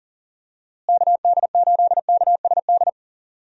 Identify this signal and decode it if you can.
Morse code, heard as a keyed tone, about 40 wpm, 700 Hz. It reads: KD8KSD
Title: USA Call Signs - 6 Characters - Repeated 2x - 40wpm